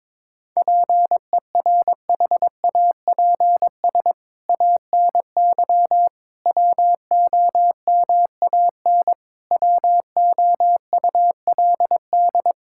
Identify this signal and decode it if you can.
Morse code, heard as a keyed tone, about 22 wpm, 700 Hz. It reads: PERHAPS ANY WOMAN WOULD
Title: Sentences from Top 1600 Words 22wpm